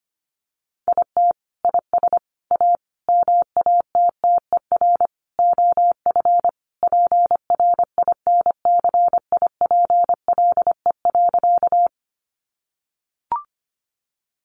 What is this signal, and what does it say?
Morse code, heard as a keyed tone, about 25 wpm, 700 Hz. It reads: IT IS A MATTER OF PRINCIPLE.
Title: Sentences from Top 1400 Words 25wpm